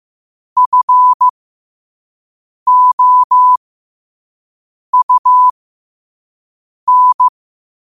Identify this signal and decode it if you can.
Morse code, heard as a keyed tone, about 15 wpm, spaced as for 5 wpm, 1000 Hz. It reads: FOUN
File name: qotdfiles / 5wpm